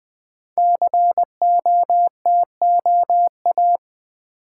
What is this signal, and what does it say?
Morse code, heard as a keyed tone, about 20 wpm, 700 Hz. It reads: COTOA
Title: Code Groups - 5 Letters - 20wpm